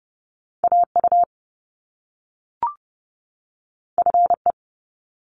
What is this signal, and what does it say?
Morse code, heard as a keyed tone, about 30 wpm, 700 Hz. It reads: AU FE
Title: Top Two Letter Combinations in Words 30wpm